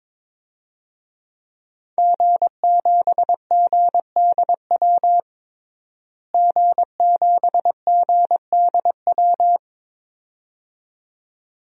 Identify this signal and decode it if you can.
Morse code, heard as a keyed tone, about 22 wpm, 700 Hz. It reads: G7GDW G7GDW
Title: DXCC-100 Call Signs - 5 Characters - Repeated 2x - 22wpm